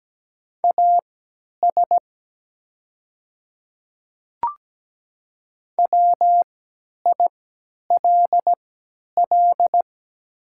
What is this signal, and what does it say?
Morse code, heard as a keyed tone, about 17 wpm, spaced as for 10 wpm, 700 Hz. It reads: AS WILL
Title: 3x Character Spacing - Top 100 Words - 17wpm